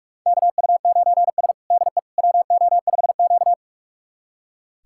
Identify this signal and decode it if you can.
Morse code, heard as a keyed tone, about 46 wpm, 700 Hz. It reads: KU9S DE WO5Q
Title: USA Callsign DE Callsign - 4 Characters - 45wpm